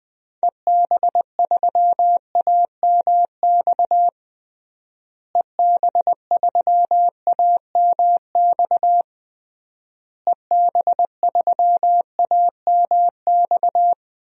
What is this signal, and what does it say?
Morse code, heard as a keyed tone, about 20 wpm, 700 Hz. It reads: EB3AMX EB3AMX EB3AMX
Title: DXCC-100 Call Signs - 6 Characters - Repeated 3x - 20wpm